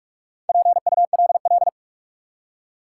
Morse code, heard as a keyed tone, about 47 wpm, 700 Hz. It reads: PULL
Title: Four Letter Words - 45wpm